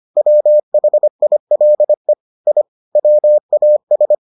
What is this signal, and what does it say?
Morse code, heard as a keyed tone, about 25 wpm, 600 Hz. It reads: WHILE I WAS